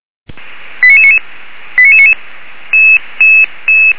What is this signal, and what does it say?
Five-tone selective call (ZVEI), heard as quick runs of five tones, tones of about 70 ms. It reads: 89999 89999